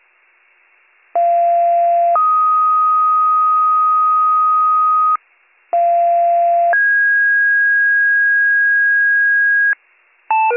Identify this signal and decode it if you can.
Two-tone page, heard as a one-second tone, then a three-second tone, about 690 then 1250 Hz, about 690 then 1650 Hz, with noise. It is heard from 1.1 to 9.7 s.